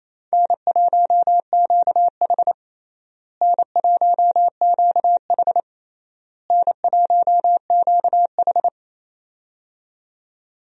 Morse code, heard as a keyed tone, about 28 wpm, 700 Hz. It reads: N1QH N1QH N1QH
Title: USA Call Signs - 4 Characters - Repeated 3x - 28wpm